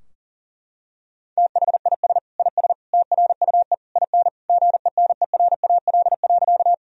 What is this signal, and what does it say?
Morse code, heard as a keyed tone, about 40 wpm, 700 Hz. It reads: THIS IS TRUE IN GENERAL.
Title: Sentences from Top 1000 Words - Review Part 3 - 40wpm